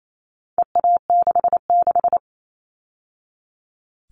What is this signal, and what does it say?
Morse code, heard as a keyed tone, about 28 wpm, 700 Hz. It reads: EA66
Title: All Cut Numbers - 4-Digit 28wpm